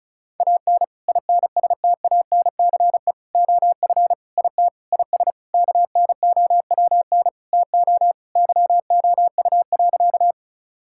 Morse code, heard as a keyed tone, about 36 wpm, 700 Hz. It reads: AN INSTANCE OF IT IS KNOWN TO YOU.
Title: Sentences from Top 1000 Words - Review Part 5 - 35wpm